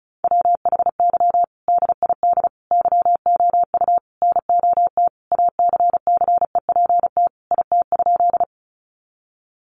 Morse code, heard as a keyed tone, about 34 wpm, 700 Hz. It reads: WHY DID YOU NOT ACCEPT IT?
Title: Sentences from Top 1300 Words 35wpm